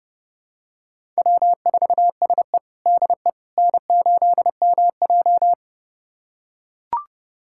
Morse code, heard as a keyed tone, about 32 wpm, 700 Hz. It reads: W4SE DE N8MJ
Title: USA Callsign DE Callsign - 4 Characters - 30wpm